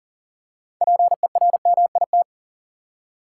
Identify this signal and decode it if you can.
Morse code, heard as a keyed tone, about 40 wpm, 700 Hz. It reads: PERMIT
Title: Six Letter Words - 40wpm